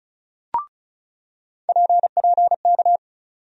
Morse code, heard as a keyed tone, about 34 wpm, 700 Hz. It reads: PPK
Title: USA Call Sign Postfixes - 3 Letters 35wpm